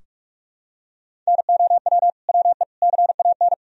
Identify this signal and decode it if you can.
Morse code, heard as a keyed tone, about 46 wpm, 700 Hz. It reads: NOW WE CAN
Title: Sentences from Top 1000 Words - Review Part 4 - 45wpm